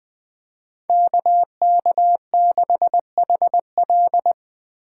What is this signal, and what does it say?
Morse code, heard as a keyed tone, about 20 wpm, 700 Hz. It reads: KK6HL
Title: Speed Racing - USA 2x2 Call Signs 17wpm